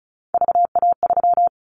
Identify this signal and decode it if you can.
Morse code, heard as a keyed tone, about 34 wpm, 700 Hz. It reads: VA3
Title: Callsign Prefixes for Top 25 DXCC - 35wpm